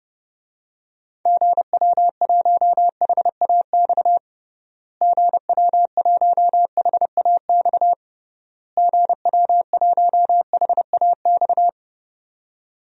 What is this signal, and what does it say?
Morse code, heard as a keyed tone, about 30 wpm, 700 Hz. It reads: GW1HAX GW1HAX GW1HAX
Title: DXCC-100 Call Signs - 6 Characters - Repeated 3x - 30wpm